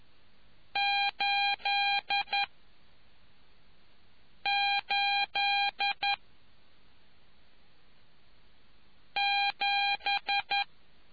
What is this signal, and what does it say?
Morse code, heard as a keyed tone, about 11 wpm, 800 Hz. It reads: 8 8 7